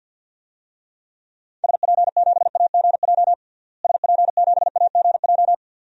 Morse code, heard as a keyed tone, about 47 wpm, 700 Hz. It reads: SP7AGJ SP7AGJ
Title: DXCC-100 Call Signs - 6 Characters - Repeated 2x - 50wpm